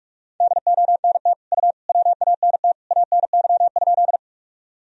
Morse code, heard as a keyed tone, about 46 wpm, 700 Hz. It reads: DONT U WANT ANY?
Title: Sentences from Top 300 Words and Abbreviations 45wpm